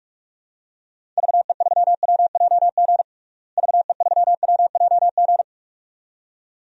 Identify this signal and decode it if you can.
Morse code, heard as a keyed tone, about 47 wpm, 700 Hz. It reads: VE3WJG VE3WJG
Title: DXCC-100 Call Signs - 6 Characters - Repeated 2x - 45wpm